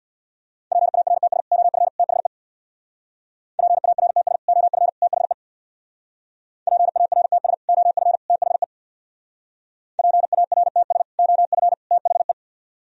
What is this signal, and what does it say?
Morse code, heard as a keyed tone, about 64 wpm, 700 Hz. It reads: PARTS OF THE PARTS OF THE PARTS OF THE PARTS OF THE